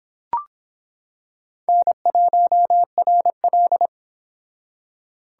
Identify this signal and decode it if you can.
Morse code, heard as a keyed tone, about 26 wpm, 700 Hz. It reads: N1RL